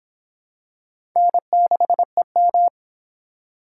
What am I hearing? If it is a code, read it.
Morse code, heard as a keyed tone, about 24 wpm, 700 Hz. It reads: N6EM